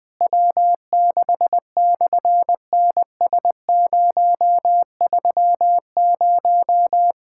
Morse code, heard as a keyed tone, about 20 wpm, 700 Hz. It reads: W6/NS030
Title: SOTA QSO Practice 20wpm